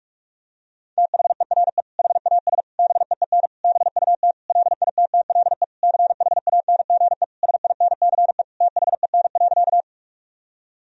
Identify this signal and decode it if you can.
Morse code, heard as a keyed tone, about 43 wpm, 700 Hz. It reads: THERE HAS BEEN BUT LITTLE CHANGE SINCE THEN.
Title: Sentences from Top 1000 Words - Review Part 1 - 45wpm